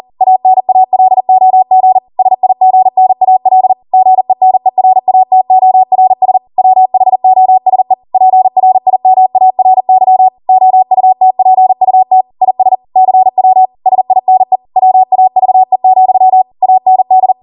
Morse code, heard as a keyed tone, about 40 wpm, 750 Hz. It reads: ANALOG SIGNAL GENERATORS WHOSE PRIMARY OUTPUT IS CW SINE WAVE, AND